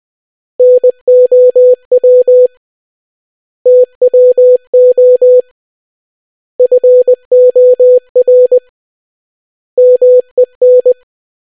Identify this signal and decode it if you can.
Morse code, heard as a keyed tone, about 20 wpm, 500 Hz. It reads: NOW TWO FOR MEN